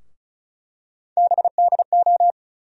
Morse code, heard as a keyed tone, about 36 wpm, 700 Hz. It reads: BDO
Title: USA Call Sign Postfixes - 3 Letters 35wpm